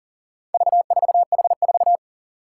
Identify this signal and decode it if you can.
Morse code, heard as a keyed tone, about 40 wpm, 700 Hz. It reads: V4H4
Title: Four Challenging Letter-Number Combinations 40wpm